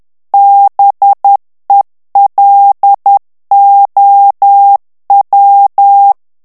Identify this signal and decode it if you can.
Morse code, heard as a keyed tone, about 10 wpm, 800 Hz. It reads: BELOW
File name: ANGLAIS texte clair 3 vitesse 10 mots minute